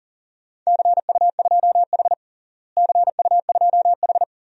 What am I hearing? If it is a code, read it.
Morse code, heard as a keyed tone, about 40 wpm, 700 Hz. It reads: CU2H CU2H
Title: DXCC-100 Call Signs - 4 Characters - Repeated 3x - 40wpm